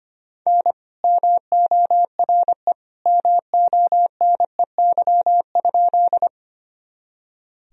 Morse code, heard as a keyed tone, about 25 wpm, 700 Hz. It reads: N MORE MONEY?
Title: Sentences from Top 400 Words and Abbreviations 25wpm